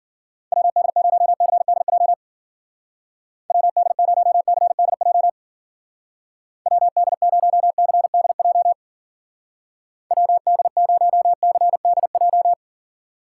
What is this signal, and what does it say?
Morse code, heard as a keyed tone, about 51 wpm, 700 Hz. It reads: WD0CDJ WD0CDJ WD0CDJ WD0CDJ